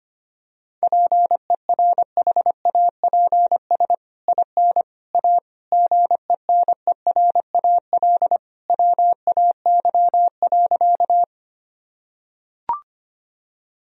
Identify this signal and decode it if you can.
Morse code, heard as a keyed tone, about 25 wpm, 700 Hz. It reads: PERHAPS IN A GENERAL WAY.
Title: Sentences from Top 1000 Words - Review Part 3 - 25wpm